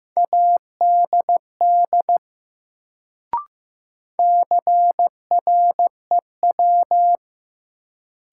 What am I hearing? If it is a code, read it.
Morse code, heard as a keyed tone, about 15 wpm, 700 Hz. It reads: ADD CREW
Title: Sentences from Top 1500 Words - New Word Drill Alt - 15wpm